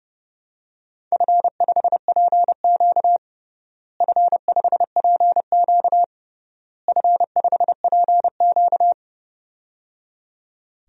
Morse code, heard as a keyed tone, about 28 wpm, 700 Hz. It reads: F5PQ F5PQ F5PQ
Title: DXCC-100 Call Signs - 4 Characters - Repeated 3x - 30wpm